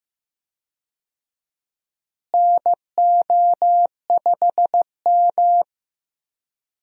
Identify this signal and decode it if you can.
Morse code, heard as a keyed tone, about 15 wpm, 700 Hz. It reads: NO5M
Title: USA Call Signs - 4 Characters - Repeated 3x - 15wpm